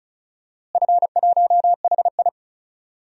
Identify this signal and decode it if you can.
Morse code, heard as a keyed tone, about 34 wpm, 700 Hz. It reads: F1HI
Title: DXCC-100 Call Signs - 4 Characters - Repeated 3x - 35wpm